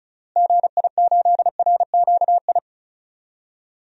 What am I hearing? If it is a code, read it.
Morse code, heard as a keyed tone, about 34 wpm, 700 Hz. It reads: GI8RQI
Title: DXCC-100 Call Signs - 6 Characters - Repeated 2x - 35wpm